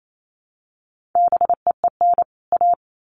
Morse code, heard as a keyed tone, about 28 wpm, 700 Hz. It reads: BEEN A
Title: Sets of 2 Words 28wpm